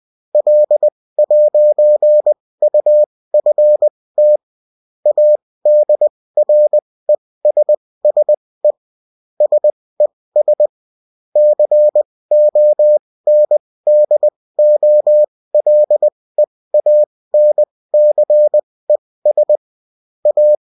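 Morse code, heard as a keyed tone, about 20 wpm, spaced as for 16 wpm, 600 Hz. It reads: L'UFT ADRESSE SES CONDOLEANCES A
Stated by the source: Morse code bulletin of a French amateur CW union, sent as text and in CW.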